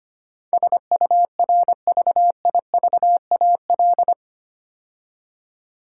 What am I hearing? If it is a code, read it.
Morse code, heard as a keyed tone, about 25 wpm, 700 Hz. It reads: SURVIVAL
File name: 8 Letter Words - First 3 Letters Given - 25wpm